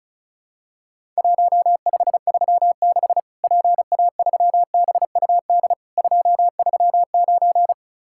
Morse code, heard as a keyed tone, about 36 wpm, 700 Hz. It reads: 1536 PA3BUD 239
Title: Worked All Europe DX Contest - QTC Exchange - 35wpm